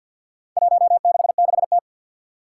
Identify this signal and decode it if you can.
Morse code, heard as a keyed tone, about 52 wpm, 700 Hz. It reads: 166T
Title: Four Challenging Letter-Number Combinations 50wpm